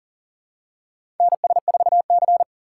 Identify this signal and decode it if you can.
Morse code, heard as a keyed tone, about 40 wpm, 700 Hz. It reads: NS4C
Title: USA Call Signs - 4 Characters - 3x Character Spacing - 40wpm